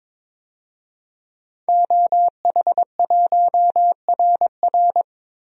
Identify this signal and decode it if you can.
Morse code, heard as a keyed tone, about 22 wpm, 700 Hz. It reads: OH1RR
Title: DXCC-100 Call Signs - 5 Characters - Repeated 3x - 22wpm